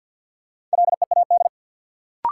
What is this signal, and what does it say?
Morse code, heard as a keyed tone, about 46 wpm, 700 Hz. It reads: LEAD
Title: Sentences from Top 500 Words - New Word Drill - 50wpm